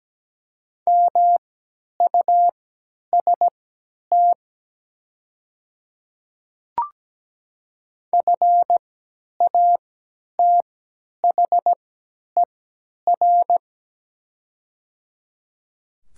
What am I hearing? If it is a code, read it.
Morse code, heard as a keyed tone, about 17 wpm, spaced as for 10 wpm, 700 Hz. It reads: MUST FATHER
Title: 3x Character Spacing - Top 300 Words - 17wpm